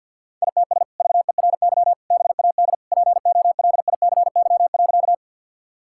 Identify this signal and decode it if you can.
Morse code, heard as a keyed tone, about 50 wpm, 700 Hz. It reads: ITS VERY BAD POLICY.
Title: Sentences from Top 1000 Words - Review Part 5 - 50wpm